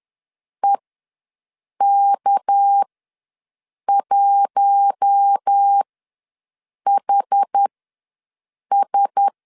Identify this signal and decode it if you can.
Morse code, heard as a keyed tone, about 10 wpm, spaced as for 6 wpm, 800 Hz. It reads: EK1HS
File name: LEZIONI / 6 PKX0 / 6 PK ws 10wpm 3745